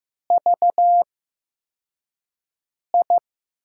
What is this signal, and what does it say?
Morse code, heard as a keyed tone, about 15 wpm, spaced as for 4 wpm, 700 Hz. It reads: VI
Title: USA Call Signs - 8x Character Spacing - 15wpm